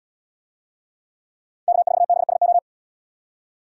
Morse code, heard as a keyed tone, about 77 wpm, 700 Hz. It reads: K4BSO